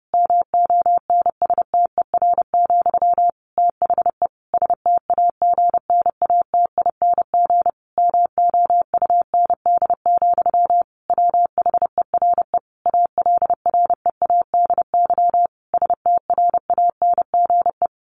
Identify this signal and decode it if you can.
Morse code, heard as a keyed tone, about 30 wpm, 700 Hz. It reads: MONSTER, THE STAGNATING MOUND, WHERE ALREADY STRANGE
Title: A Sound Of Thunder 30wpm